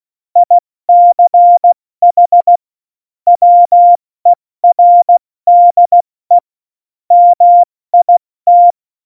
Morse code, heard as a keyed tone, about 16 wpm, spaced as for 14 wpm, 700 Hz. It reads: ICH WERDE MIT
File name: Morsekurs / Witze / Witz 23. Oktober 2018, 70 F80